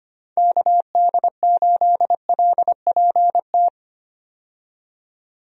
Morse code, heard as a keyed tone, about 25 wpm, 700 Hz. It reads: KD8LPT
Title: USA Call Signs - 6 Characters - 25wpm